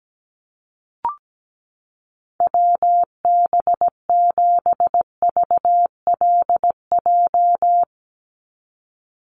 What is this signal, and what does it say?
Morse code, heard as a keyed tone, about 17 wpm, 700 Hz. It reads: WB7VLJ
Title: USA Call Signs 17wpm